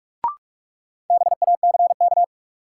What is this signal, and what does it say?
Morse code, heard as a keyed tone, about 43 wpm, 700 Hz. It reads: BACK
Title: Sentences from Top 200 Words - New Word Drill - 45wpm